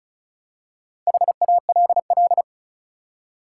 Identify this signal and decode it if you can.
Morse code, heard as a keyed tone, about 36 wpm, 700 Hz. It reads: HALL
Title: Four Letter Words - 35wpm